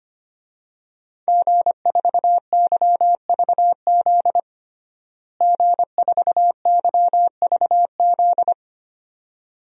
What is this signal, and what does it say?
Morse code, heard as a keyed tone, about 25 wpm, 700 Hz. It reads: G4YVZ G4YVZ
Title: DXCC-100 Call Signs - 5 Characters - Repeated 2x - 25wpm